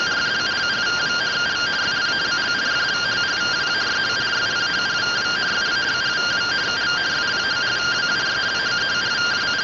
Two-tone teleprinter signal, about 170 Hz shift.